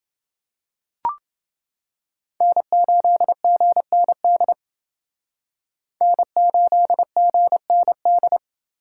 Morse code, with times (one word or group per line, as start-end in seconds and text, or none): 2.4-4.5 N8GND
6.0-8.4 N8GND